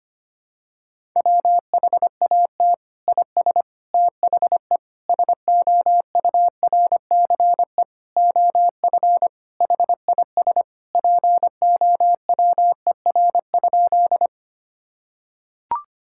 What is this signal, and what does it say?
Morse code, heard as a keyed tone, about 25 wpm, 700 Hz. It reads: WHAT IS THE SOURCE OF HIS POWER?
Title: Sentences from Top 1000 Words - Review Part 5 - 25wpm